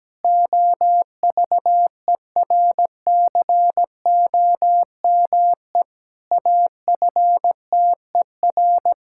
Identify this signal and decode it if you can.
Morse code, heard as a keyed tone, about 17 wpm, 700 Hz. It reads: OVERCOME AFTER
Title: Pangram Challenge v2 - 17wpm